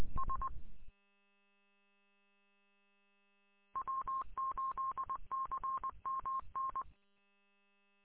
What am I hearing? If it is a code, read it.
Morse code, heard as a keyed tone, about 20 wpm, 1070 Hz. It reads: S W8CMN
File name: w8cmn- -combined